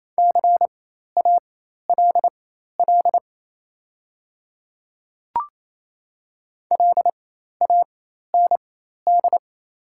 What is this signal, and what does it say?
Morse code, heard as a keyed tone, about 28 wpm, spaced as for 13 wpm, 700 Hz. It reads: CALL LAND
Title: 4x Character Spacing - Top 300 Words - 28wpm